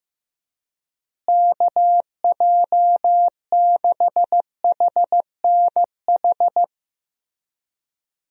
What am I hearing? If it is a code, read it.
Morse code, heard as a keyed tone, about 15 wpm, 700 Hz. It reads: KJ6HNH
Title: USA Call Signs - 6 Characters - Repeated 3x - 15wpm